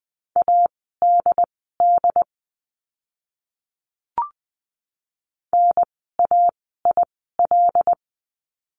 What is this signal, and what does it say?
Morse code, heard as a keyed tone, about 20 wpm, spaced as for 15 wpm, 700 Hz. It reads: ADD NAIL
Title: 075 Lesson 13f - Letter L - Words 20wpm